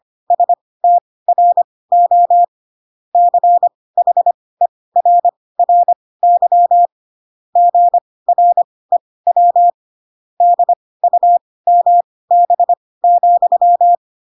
Morse code, heard as a keyed tone, about 25 wpm, spaced as for 18 wpm, 700 Hz. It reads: STRO CHERRY GREW DUMB,
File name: Pinocchio / Pinocchio ch1 12 25 ik1hge